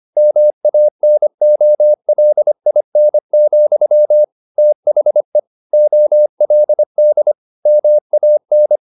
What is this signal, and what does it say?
Morse code, heard as a keyed tone, about 25 wpm, 600 Hz. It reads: MANOLIN, THE OLD MAN
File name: the old man and the sea part 1 25wpm